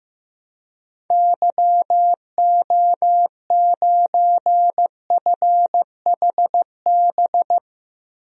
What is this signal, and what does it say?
Morse code, heard as a keyed tone, about 15 wpm, 700 Hz. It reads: YO9FHB
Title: DXCC-100 Call Signs - 6 Characters - Repeated 2x - 15wpm